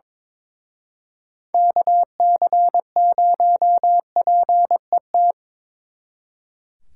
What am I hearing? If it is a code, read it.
Morse code, heard as a keyed tone, about 25 wpm, 700 Hz. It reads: KC0PET
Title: Speed Racing - USA 2x3 Call Signs 22wpm